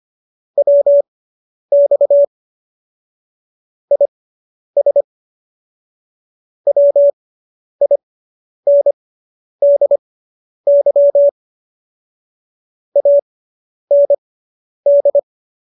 Morse code, heard as a keyed tone, about 25 wpm, spaced as for 10 wpm, 570 Hz. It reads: WX IS WINDY AND